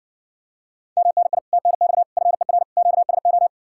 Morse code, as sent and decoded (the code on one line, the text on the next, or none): -- -. .. / - -. -..- / ..-. .-. / --.- ... ---
MNI TNX FER QSO